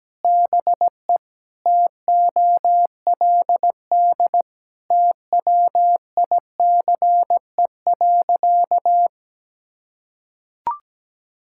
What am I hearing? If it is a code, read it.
Morse code, heard as a keyed tone, about 17 wpm, 700 Hz. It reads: BE TOLD TWICE.
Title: Sentences from Top 1600 Words 17wpm